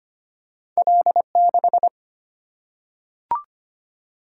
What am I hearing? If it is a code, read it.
Morse code, heard as a keyed tone, about 25 wpm, 700 Hz. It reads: L6
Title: Single-Letter Single-Number 25wpm